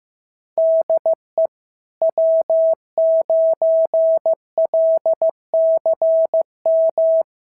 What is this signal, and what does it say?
Morse code, heard as a keyed tone, about 14 wpm, 650 Hz. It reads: DE W9LCM
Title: SOTA QSO Practice 15wpm